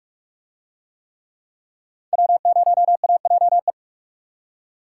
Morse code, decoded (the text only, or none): W0AJE